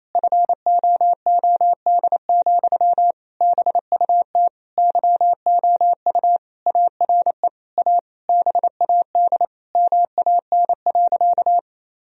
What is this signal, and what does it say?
Morse code, heard as a keyed tone, about 28 wpm, 700 Hz. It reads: FOOD, BUT YOU ARE A BAD MAN.
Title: Sentences from Top 1000 Words - Review Part 1 - 28wpm